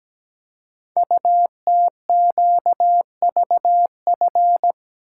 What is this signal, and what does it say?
Morse code, heard as a keyed tone, about 17 wpm, 700 Hz. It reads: UTQVF
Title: Code Groups - 5 Letters - 17wpm